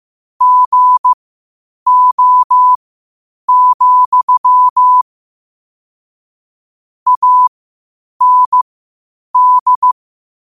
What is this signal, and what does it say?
Morse code, heard as a keyed tone, about 15 wpm, spaced as for 8 wpm, 1000 Hz. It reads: GO, AND
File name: qotdfiles / 8wpm